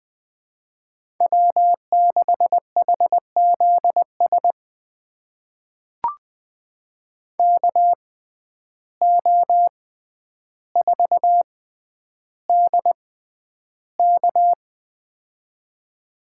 Morse code, heard as a keyed tone, about 20 wpm, spaced as for 10 wpm, 700 Hz. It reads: W6HZS KO4DK
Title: USA Call Signs - 5 Characters - 6x Character Spacing - 20wpm